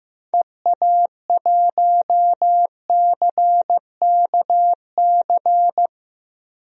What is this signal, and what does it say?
Morse code, heard as a keyed tone, about 15 wpm, 700 Hz. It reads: EA1CKC
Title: DXCC-100 Call Signs - 6 Characters - 15wpm